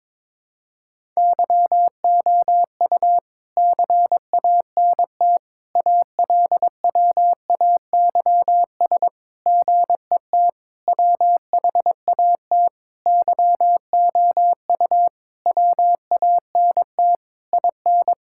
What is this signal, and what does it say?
Morse code, heard as a keyed tone, about 22 wpm, 700 Hz. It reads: YOU CANT ALWAYS GET WHAT YOU WANT IN
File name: Sentences with 7-Word Phrases - 22wpm